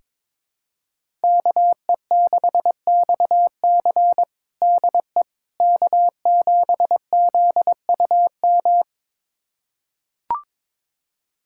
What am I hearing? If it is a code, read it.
Morse code, heard as a keyed tone, about 22 wpm, 700 Hz. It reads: KE6XC DE K7ZUM
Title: USA Callsign DE Callsign - 5 Characters - 22wpm